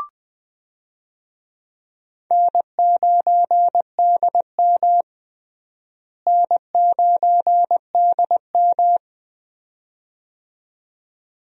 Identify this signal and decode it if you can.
Morse code, heard as a keyed tone, about 20 wpm, 700 Hz. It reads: N9DM N9DM
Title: USA Call Signs - 4 Characters - Repeated 2x - 20wpm